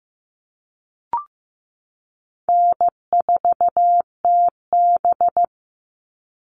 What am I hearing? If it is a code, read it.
Morse code, heard as a keyed tone, about 14 wpm, 700 Hz. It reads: N4TB
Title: Easy Call Signs Repeated 2x - 15wpm